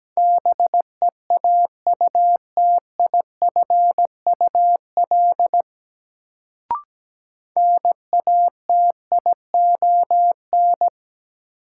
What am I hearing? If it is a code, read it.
Morse code, heard as a keyed tone, about 17 wpm, 700 Hz. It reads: BEAUTIFUL NATION
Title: Sentences from Top 900 Words - New Word Drill - 17wpm